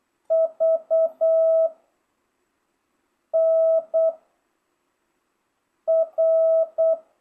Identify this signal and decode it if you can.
Morse code, heard as a keyed tone, about 8 wpm, 660 Hz. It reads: V N R